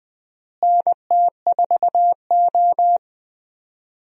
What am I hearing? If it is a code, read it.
Morse code, heard as a keyed tone, about 20 wpm, 700 Hz. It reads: NT4O
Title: USA Call Signs - 4 Characters - Repeated 2x - 20wpm